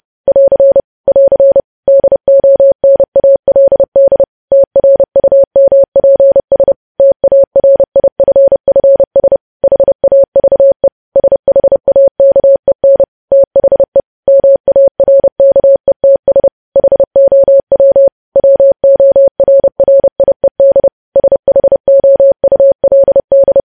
Morse code, heard as a keyed tone, about 30 wpm, 550 Hz. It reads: <AR> <AR> DONALD TRUMPS TARIFFS HAVE SHAKEN THE MARKETS HOW WORRIED SHOULD
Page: news headlines in morse code at 30 wpm